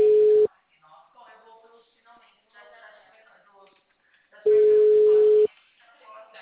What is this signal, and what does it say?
Ringback tone (Europe, 425 Hz), with speech.
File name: out- -2014- - - .4735